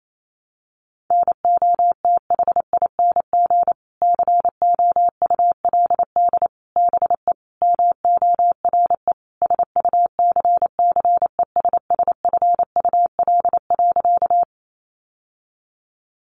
Morse code, heard as a keyed tone, about 28 wpm, 700 Hz. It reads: NOTHING COULD BE MORE SUCCESSFUL.
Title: Sentences from Top 900 Words 28wpm